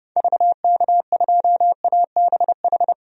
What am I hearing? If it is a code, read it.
Morse code, heard as a keyed tone, about 29 wpm, 700 Hz. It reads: VK2ABH
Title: DXCC-100 Call Signs - 6 Characters - Repeated 2x - 30wpm